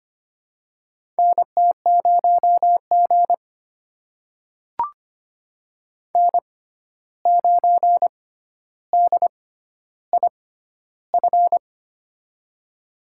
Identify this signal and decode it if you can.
Morse code, heard as a keyed tone, about 25 wpm, spaced as for 12 wpm, 700 Hz. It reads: NT0G N9DIF
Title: USA Call Signs - 6x Character Spacing - 25wpm